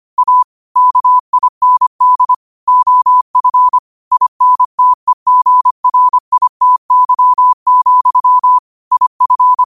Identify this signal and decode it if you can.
Morse code, heard as a keyed tone, about 25 wpm, 1000 Hz. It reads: A KIND OF INTEGRITY, IF
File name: qotdfiles / 25wpm